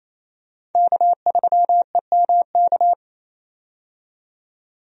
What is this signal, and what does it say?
Morse code, heard as a keyed tone, about 28 wpm, 700 Hz. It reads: K3EMK
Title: USA Call Signs - 5 Characters - 28wpm